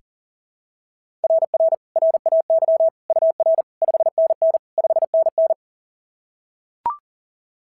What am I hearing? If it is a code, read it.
Morse code, heard as a keyed tone, about 40 wpm, 660 Hz. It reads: RR RAY UR 5NN 5NN
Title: SOTA QSO Practice 40wpm